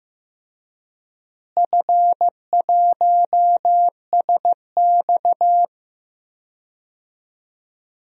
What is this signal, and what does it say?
Morse code, heard as a keyed tone, about 16 wpm, 700 Hz. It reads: F1SX
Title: DXCC-100 Call Signs - 4 Characters - Repeated 3x - 15wpm